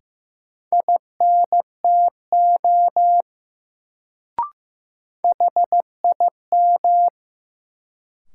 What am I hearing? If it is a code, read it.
Morse code, heard as a keyed tone, about 15 wpm, 700 Hz. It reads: INTO HIM
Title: Sets of 2 Words - New Word Drill - 15wpm